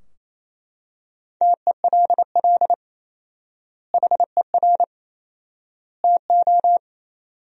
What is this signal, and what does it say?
Morse code, heard as a keyed tone, about 28 wpm, 700 Hz. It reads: TELL HER TO
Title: 4x Word-Spacing - Sentences from Top 300 Words - 28wpm